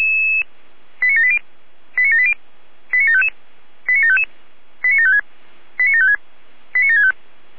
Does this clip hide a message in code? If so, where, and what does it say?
Yes: five-tone selective call (ZVEI) from 1.0 to 7.1 s, heard as quick runs of five tones, tones of about 70 ms. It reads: 89780 89780 89755 89755 89756 89756 89765